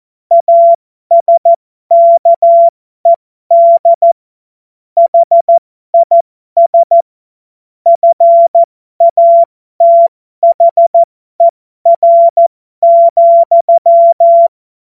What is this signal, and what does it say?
Morse code, heard as a keyed tone, about 14 wpm, spaced as for 12 wpm, 680 Hz. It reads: ASKED HIS FATHER,
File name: Morsekurs / Witze